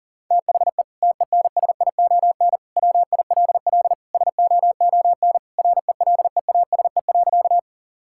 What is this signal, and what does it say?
Morse code, heard as a keyed tone, about 40 wpm, 700 Hz. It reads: THE TENSION WILL SOON RELEASE.
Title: Sentences from Top 1600 Words 40wpm